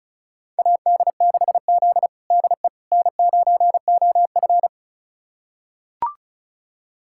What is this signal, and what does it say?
Morse code, heard as a keyed tone, about 36 wpm, 700 Hz. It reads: AD6Z DE N9OF
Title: USA Callsign DE Callsign - 4 Characters - 35wpm